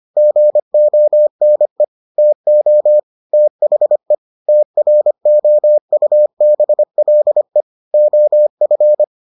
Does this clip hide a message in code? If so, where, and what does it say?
Yes: Morse code from 0.2 to 9.0 s, heard as a keyed tone, about 25 wpm, 600 Hz. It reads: GONE TO THE TROUBLE OF